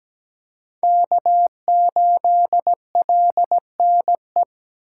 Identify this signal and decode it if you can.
Morse code, heard as a keyed tone, about 17 wpm, 700 Hz. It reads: K8LNE
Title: USA Call Signs - 2x Character Spacing - 17wpm